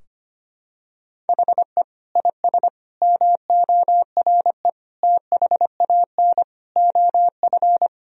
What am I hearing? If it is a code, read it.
Morse code, heard as a keyed tone, about 25 wpm, 700 Hz. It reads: HE IS MORE THAN OF
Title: Sentences from Top 1000 Words - Review Part 3 - 25wpm